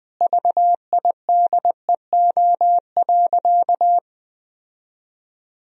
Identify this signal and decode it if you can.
Morse code, heard as a keyed tone, about 20 wpm, 700 Hz. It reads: VIDEO.
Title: Sentences from Top 1000 Words - Review Part 4 - 20wpm